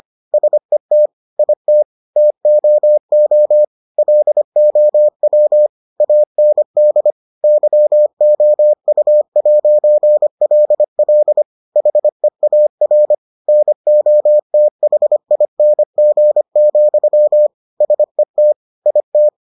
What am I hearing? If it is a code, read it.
Morse code, heard as a keyed tone, about 25 wpm, 600 Hz. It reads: SET IT TOO LOW AND YOU'LL HEAR NOTHING, SET IT